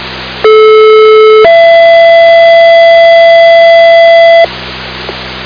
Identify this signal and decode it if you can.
Two-tone page, heard as a one-second tone, then a three-second tone, about 430 then 670 Hz, with noise. It is heard from 0.4 to 4.4 s.